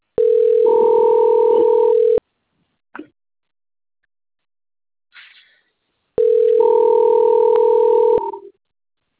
Ringback tone (North America), 440 + 480 Hz.
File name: out- - - - - .6900